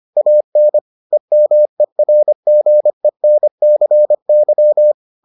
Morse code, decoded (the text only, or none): AN EMERGENCY